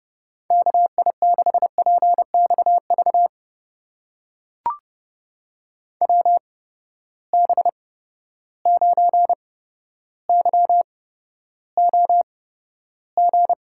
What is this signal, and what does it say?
Morse code, heard as a keyed tone, about 30 wpm, spaced as for 13 wpm, 700 Hz. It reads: KI6PXV WB9YOG
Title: USA Call Signs - 6 Characters - 8x Character Spacing - 30wpm